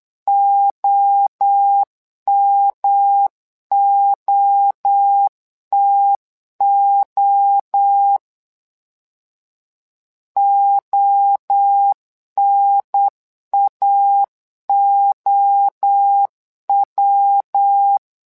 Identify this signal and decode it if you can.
Morse code, heard as a keyed tone, about 8 wpm, 800 Hz. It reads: OMOTO ONAOW
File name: LEZIONI / 3 MWRO / 3 MWRO 8wpm 2606